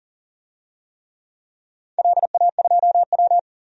Morse code, heard as a keyed tone, about 40 wpm, 700 Hz. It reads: LA2W
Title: DXCC-100 Call Signs - 4 Characters - 40wpm